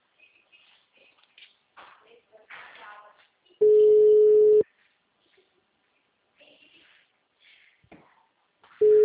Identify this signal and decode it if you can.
Ringback tone (Europe, 425 Hz), with speech.
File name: out- -2008- - - .62768